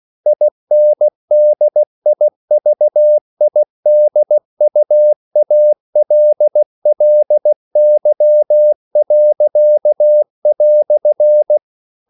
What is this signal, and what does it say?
Morse code, heard as a keyed tone, about 16 wpm, 600 Hz. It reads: INDIVIDUALLY."
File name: CW / Practice / EnglishQuotations0416